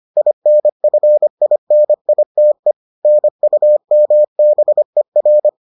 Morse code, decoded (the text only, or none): INFINITE NUMBER